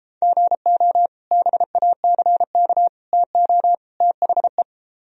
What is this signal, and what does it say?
Morse code, heard as a keyed tone, about 33 wpm, 700 Hz. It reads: GO BACK TO THE